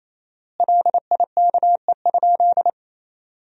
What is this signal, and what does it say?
Morse code, heard as a keyed tone, about 28 wpm, 700 Hz. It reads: LIKE?
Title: 4x Word-Spacing - Sentences from Top 100 Words - 28wpm